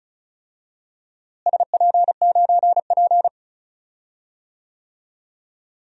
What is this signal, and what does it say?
Morse code, heard as a keyed tone, about 34 wpm, 700 Hz. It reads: SP9P
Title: DXCC-100 Call Signs - 4 Characters - Repeated 2x - 35wpm